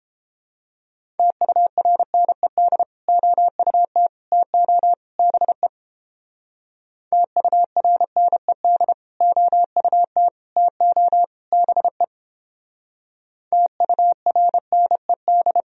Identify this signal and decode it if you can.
Morse code, heard as a keyed tone, about 29 wpm, 700 Hz. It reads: TURNED OUT TO BE TURNED OUT TO BE TURNED